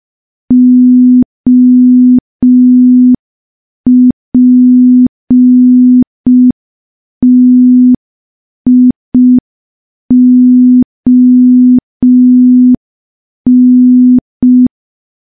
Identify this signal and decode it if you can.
Morse code, heard as a keyed tone, about 5 wpm, 250 Hz. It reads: OPTION
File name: morse / morse-training-05WPM